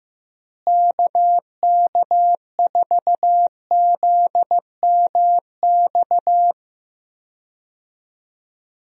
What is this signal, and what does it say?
Morse code, heard as a keyed tone, about 15 wpm, 700 Hz. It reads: KK4ZMX
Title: USA Call Signs - 6 Characters - Repeated 2x - 15wpm